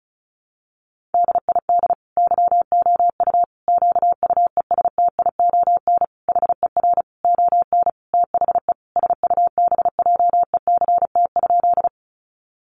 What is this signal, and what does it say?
Morse code, heard as a keyed tone, about 34 wpm, 700 Hz. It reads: DID YOU QUESTION HER ON THE SUBJECT?
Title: Sentences from Top 800 Words 35wpm